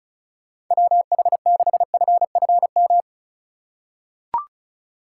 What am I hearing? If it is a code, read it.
Morse code, heard as a keyed tone, about 36 wpm, 700 Hz. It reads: WH6FFM
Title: USA Call Signs - 6 Characters - 4x Character Spacing - 35wpm